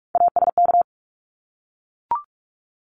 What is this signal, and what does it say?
Morse code, heard as a keyed tone, about 44 wpm, 700 Hz. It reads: ASK
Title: ED Suffix - Top 100 Words with Base Words 45wpm